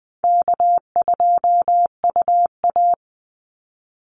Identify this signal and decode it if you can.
Morse code, heard as a keyed tone, about 20 wpm, 700 Hz. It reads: K2UA
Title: Call Signs Repeated 3x - 20wpm